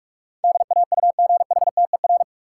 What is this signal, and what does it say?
Morse code, heard as a keyed tone, about 47 wpm, 700 Hz. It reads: DAUGHTER
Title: Sentences from Top 1500 Words - New Word Drill Alt - 45wpm